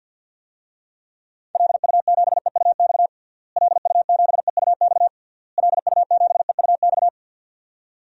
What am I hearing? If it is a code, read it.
Morse code, heard as a keyed tone, about 50 wpm, 700 Hz. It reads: LU7EUX LU7EUX LU7EUX
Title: DXCC-100 Call Signs - 6 Characters - Repeated 3x - 50wpm